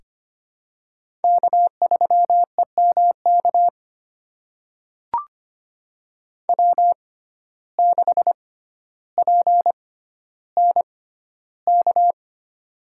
Morse code, heard as a keyed tone, about 25 wpm, spaced as for 13 wpm, 700 Hz. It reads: K3EMK W6PNK
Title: USA Call Signs - 5 Characters - 6x Character Spacing - 25wpm